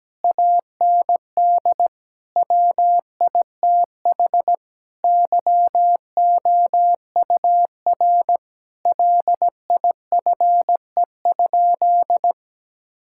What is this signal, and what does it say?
Morse code, heard as a keyed tone, about 17 wpm, 700 Hz. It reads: AND WITH YOUR LIFE?
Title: Sentences from Top 1000 Words - Review Part 6 - 17wpm